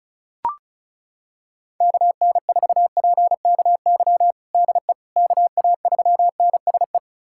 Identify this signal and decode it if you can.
Morse code, heard as a keyed tone, about 34 wpm, 700 Hz. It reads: KN4PKY DE KA3NSE
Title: USA Callsign DE Callsign - 6 Characters - 35wpm